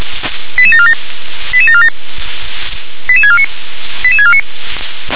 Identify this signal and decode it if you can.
Five-tone selective call (ZVEI), heard as quick runs of five tones, tones of about 70 ms. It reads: 88747 88747 88649 88649